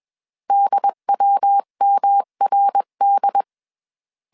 Morse code, heard as a keyed tone, about 21 wpm, 800 Hz. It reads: DWMRD
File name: CW / DET / 1 3 1500C S B 20wpm 1754